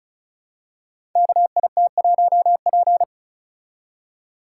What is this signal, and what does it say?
Morse code, heard as a keyed tone, about 36 wpm, 700 Hz. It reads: KIT1P
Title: Code Groups - 5 Letter:Numbers - 35wpm